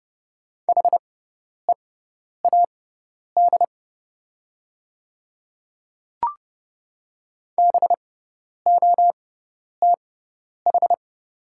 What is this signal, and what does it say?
Morse code, heard as a keyed tone, about 30 wpm, spaced as for 10 wpm, 700 Hz. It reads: HEAD BOTH
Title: 6x Character Spacing - Top 300 Words - 30wpm